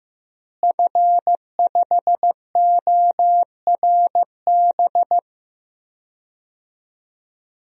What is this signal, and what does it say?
Morse code, heard as a keyed tone, about 15 wpm, 700 Hz. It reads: F5ORB
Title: DXCC-100 Call Signs - 5 Characters - Repeated 2x - 15wpm